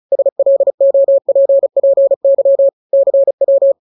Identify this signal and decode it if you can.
Morse code, heard as a keyed tone, about 34 wpm, 550 Hz. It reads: SLOPPY CW
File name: jokes / joke 24 35 0000